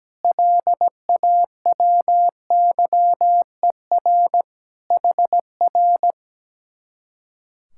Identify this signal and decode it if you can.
Morse code, heard as a keyed tone, about 17 wpm, 700 Hz. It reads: LAWYER HR
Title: Speed-Racing Common QSO Phrases - 17wpm